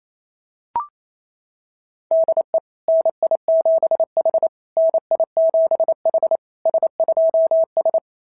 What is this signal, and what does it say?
Morse code, heard as a keyed tone, about 28 wpm, 690 Hz. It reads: DE NI7H NI7H S2S
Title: SOTA QSO Practice 28wpm